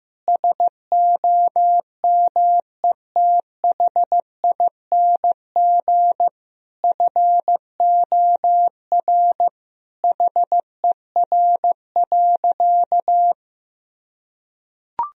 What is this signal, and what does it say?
Morse code, heard as a keyed tone, about 15 wpm, 700 Hz. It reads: SOMETHING FOR HER.
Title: Sentences from Top 1000 Words - Review Part 1 - 15wpm